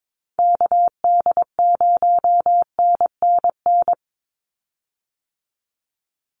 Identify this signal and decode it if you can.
Morse code, heard as a keyed tone, about 22 wpm, 700 Hz. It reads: KD0NNN
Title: USA Call Signs 22wpm